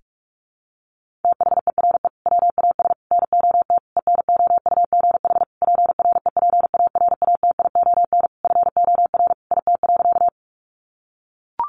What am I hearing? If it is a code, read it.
Morse code, heard as a keyed tone, about 44 wpm, 700 Hz. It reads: THERE WAS NOT ENOUGH PREPARATION FOR IT.
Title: Sentences from Top 1400 Words 45wpm